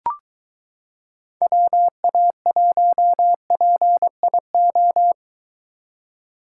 Morse code, heard as a keyed tone, about 23 wpm, 700 Hz. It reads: WA1PIO